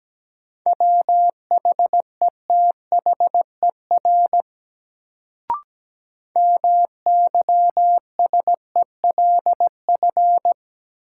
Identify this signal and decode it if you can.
Morse code, heard as a keyed tone, about 17 wpm, 700 Hz. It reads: WHETHER MYSELF
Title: Sentences from Top 1300 Words - New Word Drill Alt - 17wpm